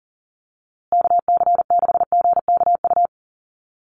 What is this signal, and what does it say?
Morse code, heard as a keyed tone, about 40 wpm, 700 Hz. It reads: KC6GKU
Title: USA Call Signs Repeated 3x 40wpm